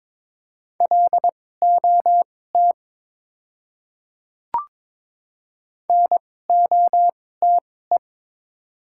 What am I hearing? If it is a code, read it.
Morse code, heard as a keyed tone, about 22 wpm, spaced as for 16 wpm, 700 Hz. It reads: LOT NOTE
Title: 2x Character Spacing - Top 500 Words - 22wpm